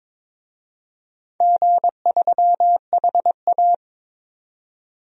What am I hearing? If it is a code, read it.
Morse code, heard as a keyed tone, about 22 wpm, 700 Hz. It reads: G3HA
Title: DXCC-100 Call Signs - 4 Characters - 22wpm